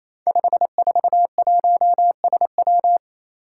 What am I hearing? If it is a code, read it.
Morse code, heard as a keyed tone, about 29 wpm, 700 Hz. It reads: 541SW
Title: Five Challenging Letter-Number Combinations 28wpm